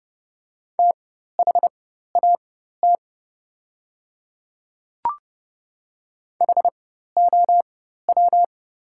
Morse code, heard as a keyed tone, about 30 wpm, spaced as for 14 wpm, 700 Hz. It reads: THAT HOW
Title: 4x Character Spacing - Top 100 Words - 30wpm